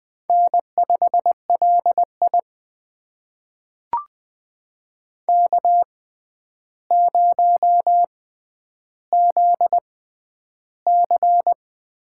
Morse code, heard as a keyed tone, about 20 wpm, spaced as for 10 wpm, 700 Hz. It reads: N5LI K0ZC
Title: USA Call Signs - 6x Character Spacing - 20wpm